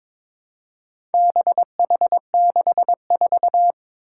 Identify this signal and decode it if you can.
Morse code, heard as a keyed tone, about 22 wpm, 700 Hz. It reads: BH64
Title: Four Challenging Letter-Number Combinations 22wpm